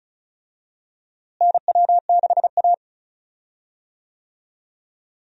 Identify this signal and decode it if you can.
Morse code, heard as a keyed tone, about 36 wpm, 700 Hz. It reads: NW6A
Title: USA Call Signs - 4 Characters - Repeated 2x - 35wpm